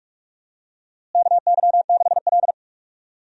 Morse code, heard as a keyed tone, about 46 wpm, 700 Hz. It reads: KY6L
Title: USA Call Signs - 4 Characters - Repeated 2x - 45wpm